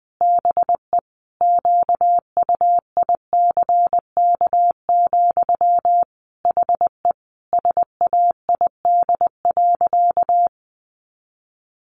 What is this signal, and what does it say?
Morse code, heard as a keyed tone, about 20 wpm, 700 Hz. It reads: BE QUICK, HE SAID.
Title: Sentences from Top 1500 Words 20wpm